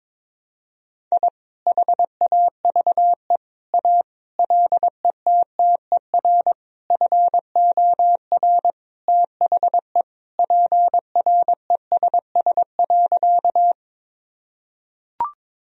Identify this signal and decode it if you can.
Morse code, heard as a keyed tone, about 22 wpm, 700 Hz. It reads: I HAVE A LETTER FOR THE PRESS.
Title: Sentences from Top 1000 Words - Review Part 1 - 22wpm